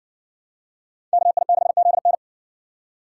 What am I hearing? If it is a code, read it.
Morse code, heard as a keyed tone, about 56 wpm, 700 Hz. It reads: KI6ZN